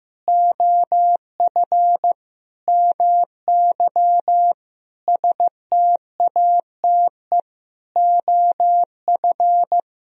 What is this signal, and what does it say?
Morse code, heard as a keyed tone, about 15 wpm, 700 Hz. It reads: OF MY STATE OF
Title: Sentences from Top 1000 Words - Review Part 2 - 15wpm